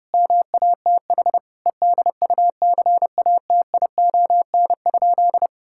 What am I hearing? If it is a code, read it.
Morse code, heard as a keyed tone, about 30 wpm, 700 Hz. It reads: MATH EDUCATION?
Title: Sentences from Top 1000 Words - Review Part 6 - 30wpm